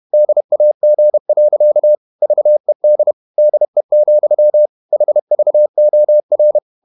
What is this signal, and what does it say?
Morse code, heard as a keyed tone, about 31 wpm, 600 Hz. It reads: DAG. VED DE, HVOR
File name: Ferie QRQ 31wpm